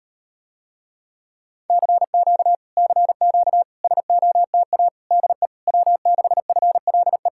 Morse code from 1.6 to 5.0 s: CQ CQ SOTA

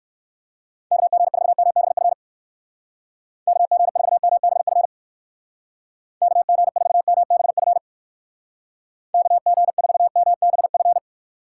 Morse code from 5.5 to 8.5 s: KG4MBF

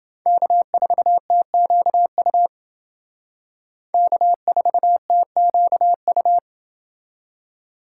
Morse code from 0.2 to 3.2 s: K4TQU

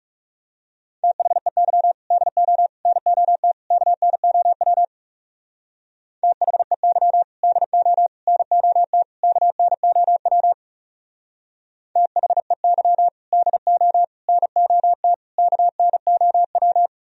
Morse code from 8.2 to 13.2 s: NOT KNOW THEY

